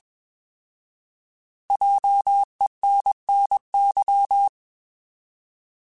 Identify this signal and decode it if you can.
Morse code, heard as a keyed tone, about 21 wpm, 800 Hz. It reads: JENNY
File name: Alphabetical / Alphabetical mp3s / Js @20